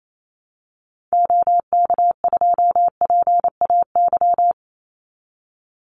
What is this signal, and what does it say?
Morse code, heard as a keyed tone, about 28 wpm, 700 Hz. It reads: OK2PAY
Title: Call Signs Repeated 3x - 28wpm